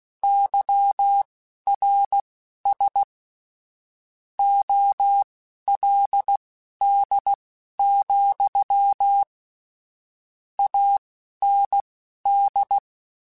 Morse code, heard as a keyed tone, about 16 wpm, spaced as for 11 wpm, 800 Hz. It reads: YRS OLD, AND
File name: QSOs / QSOs-mp3s / kc7haa@10